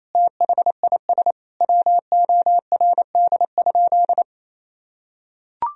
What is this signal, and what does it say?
Morse code, heard as a keyed tone, about 28 wpm, 700 Hz. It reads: THIS WORD?
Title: Sentences from Top 1000 Words - Review Part 4 - 28wpm